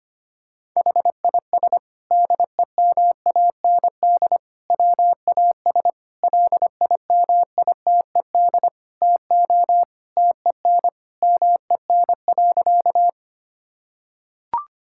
Morse code, heard as a keyed tone, about 25 wpm, 700 Hz. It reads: HIS DEMAND WAS LIMITED TO TEN MEN.
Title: Sentences from Top 1000 Words - Review Part 6 - 25wpm